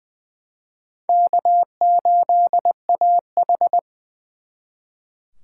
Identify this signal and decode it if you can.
Morse code, heard as a keyed tone, about 23 wpm, 700 Hz. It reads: K8AH
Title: Speed Racing - USA 1x2 Call Signs 20wpm